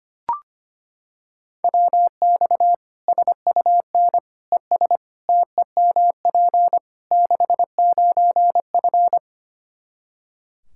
Morse code, heard as a keyed tone, about 25 wpm, 700 Hz. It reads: WX SUN ES TEMP 69F
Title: K1USN SST QSO Practice 25wpm